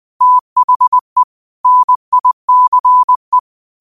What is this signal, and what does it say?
Morse code, heard as a keyed tone, about 20 wpm, 1000 Hz. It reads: THE NICE
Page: New quotes every day in morse code at 20 Words per minute.